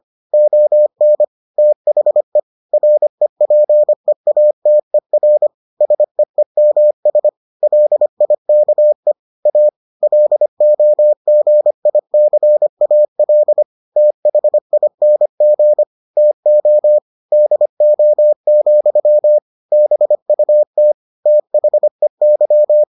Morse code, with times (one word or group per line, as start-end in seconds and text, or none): 0.3-1.2 ON
1.6-2.4 THE
2.7-5.5 REPEATER
5.8-7.3 SEEMS
7.6-9.1 LIKE
9.5-9.7 A
10.0-13.6 LOGICAL
14.0-15.8 THING
16.2-17.0 TO
17.3-19.4 DO,
19.7-20.9 BUT
21.3-22.8 THEY